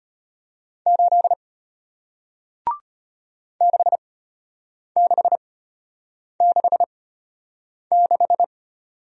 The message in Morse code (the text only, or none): 8 6666